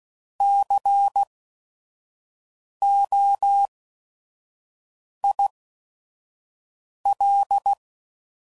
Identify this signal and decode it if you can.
Morse code, heard as a keyed tone, about 16 wpm, spaced as for 5 wpm, 800 Hz. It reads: COIL
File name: Prefixes / By First Letter / Cs